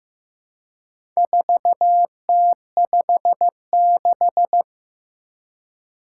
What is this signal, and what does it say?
Morse code, heard as a keyed tone, about 15 wpm, 700 Hz. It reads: 4T56
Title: Four Challenging Letter-Number Combinations 15wpm